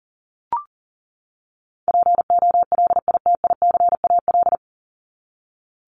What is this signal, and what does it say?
Morse code, heard as a keyed tone, about 40 wpm, 700 Hz. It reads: POLITICAL
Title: Top 501-750 Words 40wpm